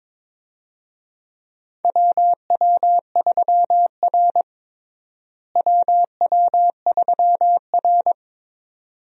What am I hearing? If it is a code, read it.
Morse code, heard as a keyed tone, about 22 wpm, 700 Hz. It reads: WW3R WW3R
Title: USA Call Signs - 4 Characters - Repeated 2x - 22wpm